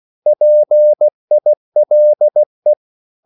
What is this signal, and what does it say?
Morse code, heard as a keyed tone, about 16 wpm, 600 Hz. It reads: PILE
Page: Korte danske ord 16wpm